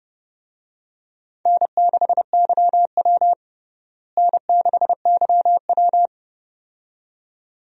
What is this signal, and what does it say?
Morse code, heard as a keyed tone, about 32 wpm, 700 Hz. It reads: N6YW N6YW
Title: USA Call Signs - 4 Characters - Repeated 2x - 30wpm